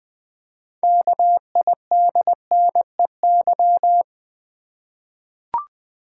Morse code, heard as a keyed tone, about 20 wpm, 700 Hz. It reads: KIDNEY
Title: 6 Letter Words - First 3 Letters Given - 20wpm